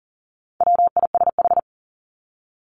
Morse code, heard as a keyed tone, about 40 wpm, 700 Hz. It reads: WISH
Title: Top 1501-1750 Words 40wpm